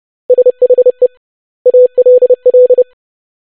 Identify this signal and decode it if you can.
Morse code, heard as a keyed tone, about 30 wpm, 500 Hz. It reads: SHE ALL
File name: basic100-25wpm